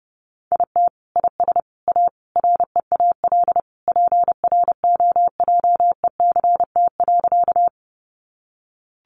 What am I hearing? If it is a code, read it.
Morse code, heard as a keyed tone, about 30 wpm, 700 Hz. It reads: IT IS A REAL PROJECT.
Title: Sentences from Top 1400 Words 30wpm